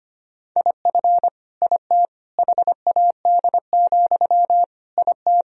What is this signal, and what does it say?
Morse code, heard as a keyed tone, about 25 wpm, 700 Hz. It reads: IF IT HAD, IT
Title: Sentences from Top 1000 Words - Review Part 3 - 25wpm